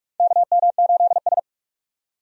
Morse code, heard as a keyed tone, about 46 wpm, 700 Hz. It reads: KM8S
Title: USA Call Signs - 4 Characters - 45wpm